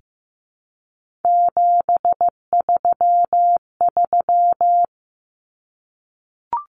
Morse code, heard as a keyed tone, about 15 wpm, 700 Hz. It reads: 733
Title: Common Cut Numbers - 3-Digit 15wpm